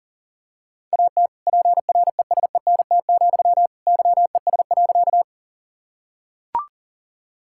Morse code, heard as a keyed tone, about 40 wpm, 700 Hz. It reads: AT PRESENT, YES.
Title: Sentences from Top 1000 Words - Review Part 5 - 40wpm